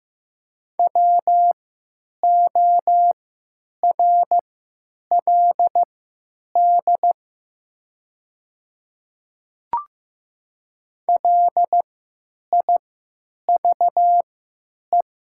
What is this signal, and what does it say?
Morse code, heard as a keyed tone, about 15 wpm, spaced as for 9 wpm, 700 Hz. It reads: WORLD LIVE
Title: 3x Character Spacing - Top 500 Words - 15wpm